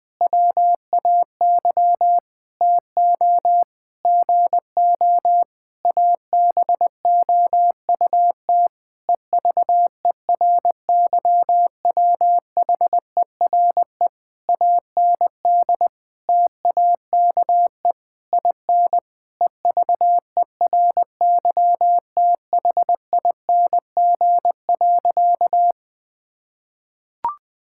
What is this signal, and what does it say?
Morse code, heard as a keyed tone, about 19 wpm, 700 Hz. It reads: WAY TO GO ABOUT EVERYWHERE AND TAKE IN EVERYTHING.
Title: Sentences from Top 1600 Words 20wpm